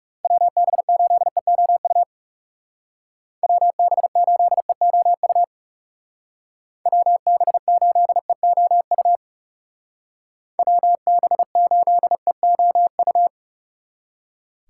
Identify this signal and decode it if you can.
Morse code, heard as a keyed tone, about 37 wpm, 700 Hz. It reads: WB8EOU WB8EOU WB8EOU WB8EOU
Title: Speed Racing - USA 2x3 Call Signs 30wpm